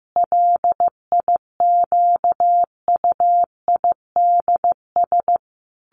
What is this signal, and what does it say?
Morse code, heard as a keyed tone, about 15 wpm, 700 Hz. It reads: LIQUIDS
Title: A Sound Of Thunder 15wpm